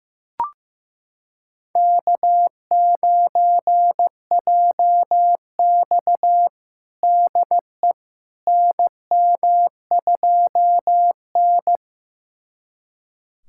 Morse code, heard as a keyed tone, about 16 wpm, 700 Hz. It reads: K9JX DE NM2N
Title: USA Callsign DE Callsign - 4 Characters - 15wpm